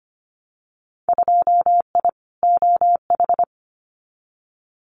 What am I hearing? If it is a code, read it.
Morse code, heard as a keyed tone, about 25 wpm, 700 Hz. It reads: 2I OH
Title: Winter Field Day Exchanges 25wpm